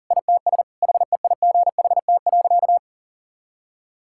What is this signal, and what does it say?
Morse code, heard as a keyed tone, about 40 wpm, 700 Hz. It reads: ITS HEIGHT.
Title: Sentences from Top 1600 Words 40wpm